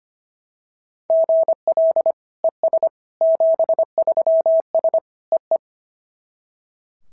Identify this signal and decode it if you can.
Morse code, heard as a keyed tone, about 25 wpm, 690 Hz. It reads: GL ES 73S EE
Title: SOTA QSO Practice 25wpm